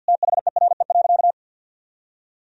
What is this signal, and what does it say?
Morse code, heard as a keyed tone, about 51 wpm, 700 Hz. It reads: THERE.
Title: Sentences from Top 1000 Words - Review Part 1 - 50wpm